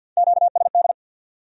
Morse code, heard as a keyed tone, about 61 wpm, 700 Hz. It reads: QSD
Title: Speed-Racing Q-Codes - 50wpm